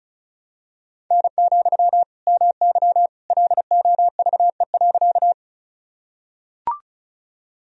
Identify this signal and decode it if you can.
Morse code, heard as a keyed tone, about 36 wpm, 700 Hz. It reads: N, MY LOVE.
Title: Sentences from Top 400 Words and Abbreviations 35wpm